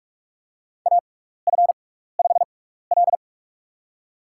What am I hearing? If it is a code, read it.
Morse code, heard as a keyed tone, about 42 wpm, spaced as for 15 wpm, 700 Hz. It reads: AF5L
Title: USA Call Signs - 4 Characters - 6x Character Spacing - 45wpm